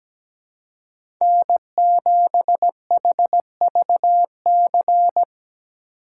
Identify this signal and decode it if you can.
Morse code, heard as a keyed tone, about 17 wpm, 700 Hz. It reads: N7HVC
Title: USA Call Signs - 3x Character Spacing - 17wpm